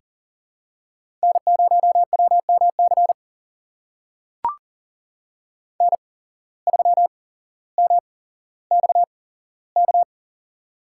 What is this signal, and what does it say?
Morse code, heard as a keyed tone, about 38 wpm, spaced as for 17 wpm, 700 Hz. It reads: N0WMC N3MXK
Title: USA Call Signs - 5 Characters - 8x Character Spacing - 40wpm